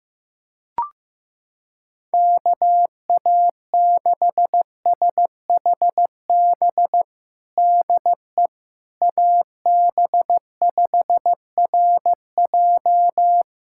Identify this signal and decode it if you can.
Morse code, heard as a keyed tone, about 15 wpm, 700 Hz. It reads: KA6SHB DE AB5RJ
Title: USA Callsign DE Callsign 15wpm